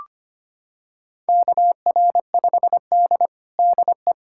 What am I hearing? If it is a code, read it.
Morse code, heard as a keyed tone, about 25 wpm, 700 Hz. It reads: KR5D DE
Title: USA Callsign DE Callsign - 4 Characters - 25wpm